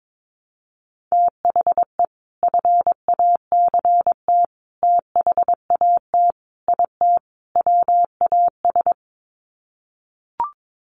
Morse code, heard as a keyed tone, about 22 wpm, 700 Hz. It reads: THE FACT THAT IT WAS
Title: Sets of 5 Words Encore2 22wpm